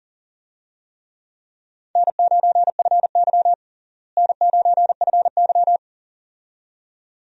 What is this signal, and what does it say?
Morse code, heard as a keyed tone, about 40 wpm, 700 Hz. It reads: N9FY N9FY
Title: USA Call Signs - 4 Characters - Repeated 2x - 40wpm